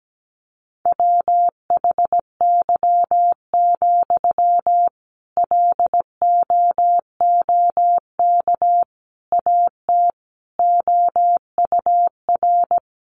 Morse code, heard as a keyed tone, about 17 wpm, 700 Hz. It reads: WHY, LOOK AT OUR
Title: Sentences from Top 600 Words 17wpm